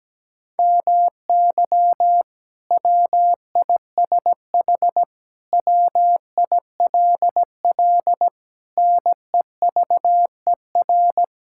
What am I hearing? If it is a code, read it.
Morse code, heard as a keyed tone, about 17 wpm, 700 Hz. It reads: MY WISH WILL NEVER
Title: Sentences from Top 1600 Words 17wpm